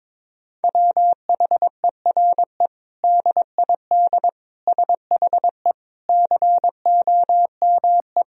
Morse code, heard as a keyed tone, about 22 wpm, 700 Hz. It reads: WHERE DID SHE COME
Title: Sentences from Top 200 Words and Abbreviations 22wpm